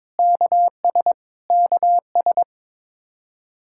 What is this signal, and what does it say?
Morse code, heard as a keyed tone, about 22 wpm, 700 Hz. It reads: KS KS
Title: K1USN SST QSO Practice 22wpm